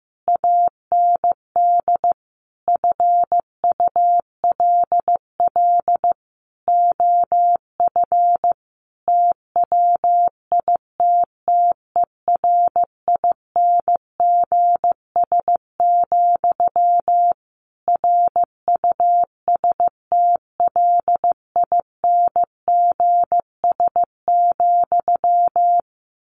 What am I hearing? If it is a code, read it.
Morse code, heard as a keyed tone, about 14 wpm, 700 Hz. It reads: AND FULL OF TWITTERINGS, RUSTLINGS,
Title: A Sound Of Thunder 15wpm